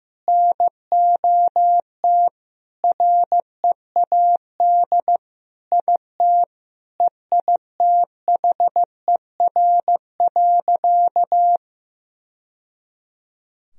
Morse code, heard as a keyed tone, about 15 wpm, 700 Hz. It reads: NOT READ IT EITHER.
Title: Sentences from Top 1000 Words - Review Part 4 - 15wpm